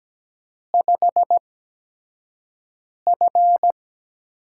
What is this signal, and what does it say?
Morse code, heard as a keyed tone, about 17 wpm, spaced as for 5 wpm, 700 Hz. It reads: 5F
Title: USA Call Signs - 4 Characters - 8x Character Spacing - 17wpm